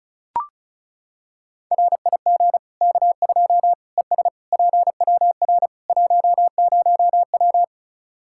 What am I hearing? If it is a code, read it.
Morse code, heard as a keyed tone, about 36 wpm, 700 Hz. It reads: RIG K2 ES PWR 10W
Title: Practice QSO - Exchange 2 - Info - 35wpm